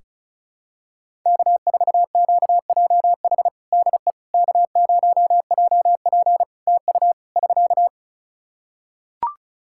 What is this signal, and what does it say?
Morse code, heard as a keyed tone, about 36 wpm, 700 Hz. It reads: K4QJH DE K0JP TU <SK>
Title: Practice QSO - Exchange 4 - Ending - 35wpm